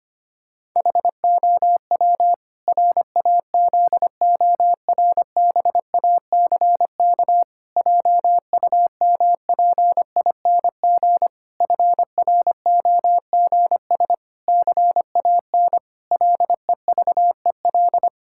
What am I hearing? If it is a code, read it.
Morse code, heard as a keyed tone, about 25 wpm, 700 Hz. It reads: HOW RAZORBACK JUMPING FROGS CAN LEVEL
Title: Pangram Challenge v2 - 25wpm